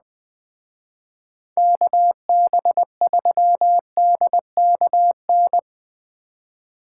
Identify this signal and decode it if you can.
Morse code, heard as a keyed tone, about 20 wpm, 700 Hz. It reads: KB3DKN
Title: Speed Racing - USA 2x3 Call Signs 17wpm